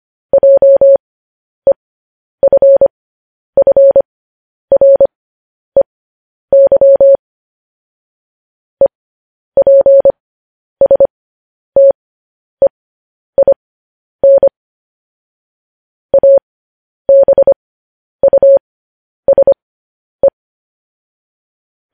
Morse code, heard as a keyed tone, about 25 wpm, spaced as for 10 wpm, 550 Hz. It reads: JEFFREY EPSTEIN ABUSE
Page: news headlines in morse code at 10 wpm